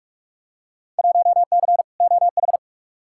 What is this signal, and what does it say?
Morse code, heard as a keyed tone, about 46 wpm, 700 Hz. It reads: 1C OH
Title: Field Day Exchanges - 45wpm